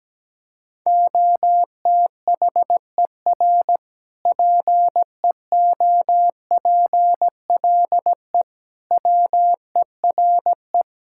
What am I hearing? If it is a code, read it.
Morse code, heard as a keyed tone, about 17 wpm, 700 Hz. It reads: OTHER PEOPLE WERE
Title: Sentences from Top 1000 Words - Review Part 5 - 17wpm